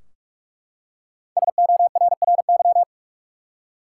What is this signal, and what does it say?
Morse code, heard as a keyed tone, about 41 wpm, 700 Hz. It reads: SORRY
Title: Five Letter Words - 45wpm